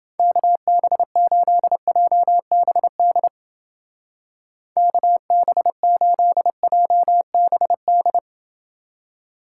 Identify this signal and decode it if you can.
Morse code, heard as a keyed tone, about 30 wpm, 700 Hz. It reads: KB8JBD KB8JBD